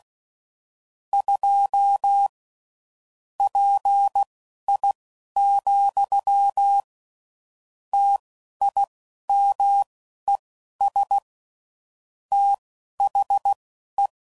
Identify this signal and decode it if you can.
Morse code, heard as a keyed tone, about 16 wpm, spaced as for 11 wpm, 800 Hz. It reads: 2 PI, TIMES THE
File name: CW Code Groups @10WPM / Technical Subjects / XC